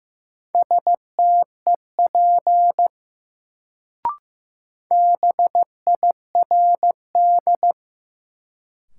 Morse code, heard as a keyed tone, about 15 wpm, 700 Hz. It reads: STEP BIRD
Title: Sentences from Top 400 Words - New Word Drill - 15wpm